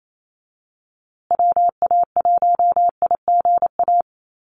Morse code, heard as a keyed tone, about 28 wpm, 700 Hz. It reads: WA1IGA
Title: USA Call Signs Repeated 3x 28wpm